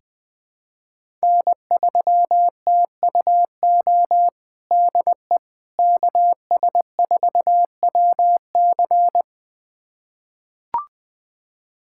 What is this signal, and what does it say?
Morse code, heard as a keyed tone, about 20 wpm, 700 Hz. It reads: N3TUO DE KS4WC
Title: USA Callsign DE Callsign - 5 Characters - 20wpm